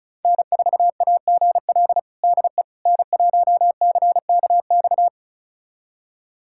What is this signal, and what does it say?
Morse code, heard as a keyed tone, about 36 wpm, 700 Hz. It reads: N4AGL DE N1CKX
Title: USA Callsign DE Callsign - 5 Characters - 35wpm